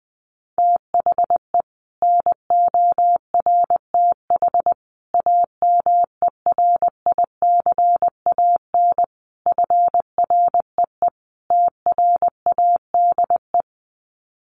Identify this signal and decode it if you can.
Morse code, heard as a keyed tone, about 20 wpm, 700 Hz. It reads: THE NORTH AMERICAN FREE TRADE
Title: Sets of 5 Words Encore 20wpm